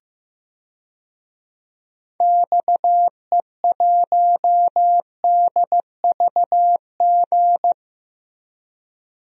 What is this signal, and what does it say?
Morse code, heard as a keyed tone, about 14 wpm, 700 Hz. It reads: XE1DVG
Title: DXCC-100 Call Signs - 6 Characters - Repeated 3x - 15wpm